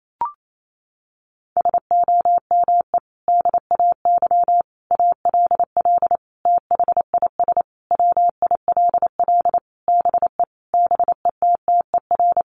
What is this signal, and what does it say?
Morse code, heard as a keyed tone, about 28 wpm, 700 Hz. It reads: SOME DAY ALL THIS WILL BE BETTER
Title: Sentences from Top 1300 Words 28wpm